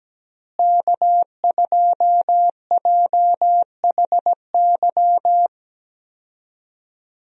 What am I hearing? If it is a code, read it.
Morse code, heard as a keyed tone, about 17 wpm, 700 Hz. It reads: K2JHY
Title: USA Call Signs - 5 Characters - Repeated 2x - 17wpm